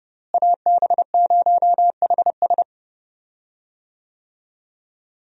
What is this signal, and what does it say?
Morse code, heard as a keyed tone, about 30 wpm, 700 Hz. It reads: AB0HS
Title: USA Call Signs - 5 Characters - Repeated 2x - 30wpm